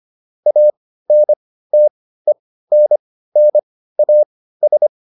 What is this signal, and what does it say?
Morse code, heard as a keyed tone, about 25 wpm, spaced as for 15 wpm, 600 Hz. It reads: ANTENNAS